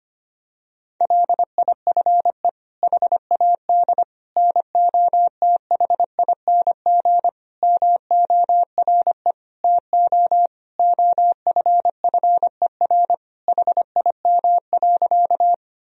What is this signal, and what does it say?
Morse code, heard as a keyed tone, about 25 wpm, 700 Hz. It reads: LIFE HAD NOTHING MORE TO OFFER HIM.
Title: Sentences from Top 1000 Words - Review Part 4 - 25wpm